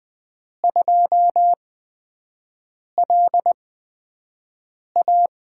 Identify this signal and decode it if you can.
Morse code, heard as a keyed tone, about 20 wpm, spaced as for 5 wpm, 700 Hz. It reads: 2LA
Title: USA Call Signs - 8x Character Spacing - 20wpm